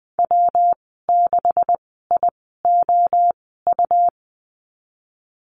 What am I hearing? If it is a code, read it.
Morse code, heard as a keyed tone, about 20 wpm, spaced as for 15 wpm, 700 Hz. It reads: W6IOU
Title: 193 Lesson 31g - Letter v - Call Signs 20wpm